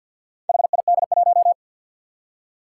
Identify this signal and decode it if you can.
Morse code, heard as a keyed tone, about 52 wpm, 700 Hz. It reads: HID1
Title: Four Challenging Letter-Number Combinations 50wpm